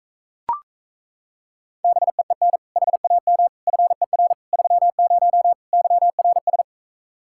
Morse code, heard as a keyed tone, about 38 wpm, 700 Hz. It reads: BEEN HAM FER 30 YRS